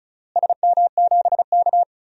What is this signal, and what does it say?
Morse code, heard as a keyed tone, about 34 wpm, 700 Hz. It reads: SM7K
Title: DXCC-100 Call Signs - 4 Characters - Repeated 3x - 35wpm